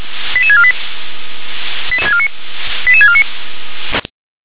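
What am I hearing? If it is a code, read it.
Five-tone selective call (ZVEI), heard as quick runs of five tones, tones of about 70 ms. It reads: 88648 88649
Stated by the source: ZVEI five-tone call 88671 received on an SDR.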